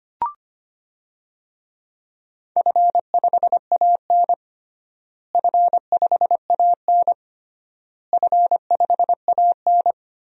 Morse code, heard as a keyed tone, about 25 wpm, 700 Hz. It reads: F5AN F5AN F5AN
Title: DXCC-100 Call Signs - 4 Characters - Repeated 3x - 25wpm